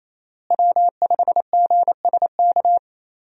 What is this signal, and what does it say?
Morse code, heard as a keyed tone, about 28 wpm, 700 Hz. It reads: W5GSK
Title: USA Call Signs - 5 Characters - Repeated 2x - 28wpm